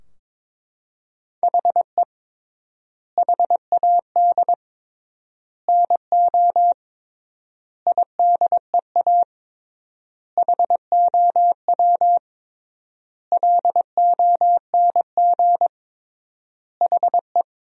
Morse code, heard as a keyed tone, about 22 wpm, 700 Hz. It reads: HE HAD NO IDEA HOW LONG HE
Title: 3x Word-Spacing - Sentences from Top 300 Words - 22wpm